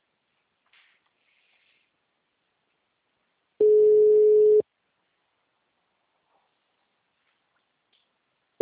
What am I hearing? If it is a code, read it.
Ringback tone (Europe, 425 Hz).
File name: out- -2010- - - .4